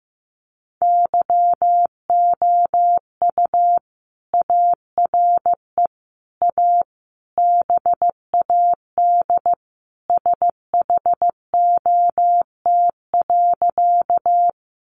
Morse code, heard as a keyed tone, about 15 wpm, 700 Hz. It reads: YOU ARE A BAD SHOT.
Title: Sentences from Top 1100 Words 15wpm